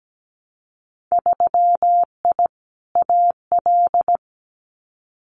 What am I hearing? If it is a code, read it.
Morse code, heard as a keyed tone, about 17 wpm, 700 Hz. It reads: 3I AL
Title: Winter Field Day Exchanges Encore 17wpm